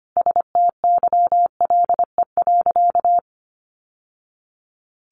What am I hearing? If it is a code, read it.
Morse code, heard as a keyed tone, about 25 wpm, 700 Hz. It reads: STYLE.
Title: Sentences from Top 800 Words 25wpm